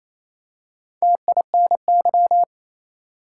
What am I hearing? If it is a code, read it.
Morse code, heard as a keyed tone, about 28 wpm, 700 Hz. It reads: TINY
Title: Four Letter Words - 28wpm